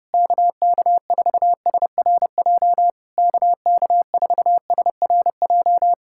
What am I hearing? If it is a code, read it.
Morse code, heard as a keyed tone, about 28 wpm, 700 Hz. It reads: KK4SRJ KK4SRJ
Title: Speed-Racing Common QSO Phrases Encore - 30wpm